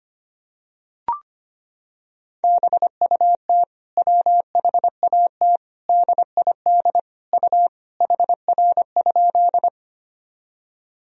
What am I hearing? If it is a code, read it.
Morse code, heard as a keyed tone, about 25 wpm, 700 Hz. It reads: BUT WHAT DID U HR?
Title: Sentences from Top 300 Words and Abbreviations 25wpm